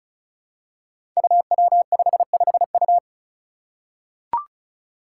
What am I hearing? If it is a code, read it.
Morse code, heard as a keyed tone, about 34 wpm, 700 Hz. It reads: UW55U
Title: Five Challenging Letter-Number Combinations 35wpm